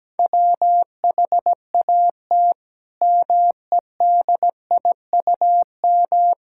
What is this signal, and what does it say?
Morse code, heard as a keyed tone, about 17 wpm, 700 Hz. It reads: WHAT MEDIUM
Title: Sentences from Top 1000 Words - Review Part 6 - 17wpm